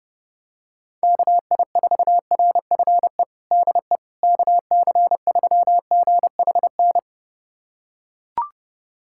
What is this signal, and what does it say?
Morse code, heard as a keyed tone, about 32 wpm, 700 Hz. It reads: KI4RFE DE KC3GHN
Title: USA Callsign DE Callsign - 6 Characters - 30wpm